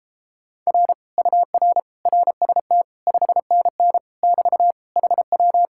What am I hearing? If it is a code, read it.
Morse code, heard as a keyed tone, about 33 wpm, 700 Hz. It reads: R UR RST 5NN <BT> HW